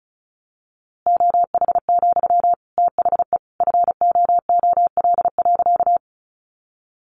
Morse code, heard as a keyed tone, about 34 wpm, 700 Hz. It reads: OH, THE FOOL.
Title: A Sound Of Thunder 35wpm